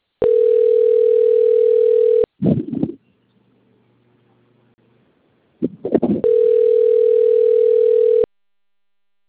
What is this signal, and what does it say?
Ringback tone (North America), 440 + 480 Hz.